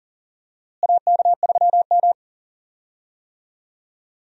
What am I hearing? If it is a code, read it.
Morse code, heard as a keyed tone, about 39 wpm, 700 Hz. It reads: AK3M
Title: USA Call Signs - 4 Characters - Repeated 2x - 40wpm